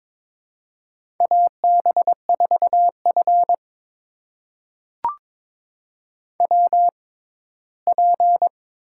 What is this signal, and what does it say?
Morse code, heard as a keyed tone, about 22 wpm, spaced as for 15 wpm, 700 Hz. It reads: AB4F WP
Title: USA Call Signs - 4 Characters - 6x Character Spacing - 22wpm